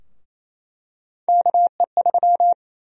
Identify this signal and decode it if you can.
Morse code, heard as a keyed tone, about 28 wpm, 700 Hz. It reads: KE3
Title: USA Call Sign Prefixes - 28wpm